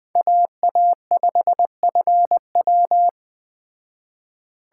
Morse code, heard as a keyed tone, about 20 wpm, 700 Hz. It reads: AA5FW
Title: Speed Racing - USA 2x2 Call Signs 17wpm